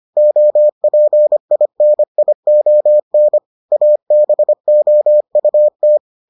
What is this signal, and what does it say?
Morse code, heard as a keyed tone, about 25 wpm, 600 Hz. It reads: OPINION ABOUT